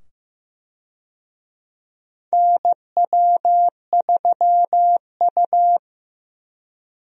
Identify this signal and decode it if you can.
Morse code, heard as a keyed tone, about 15 wpm, 700 Hz. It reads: NW3U
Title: USA Call Signs - 4 Characters - 15wpm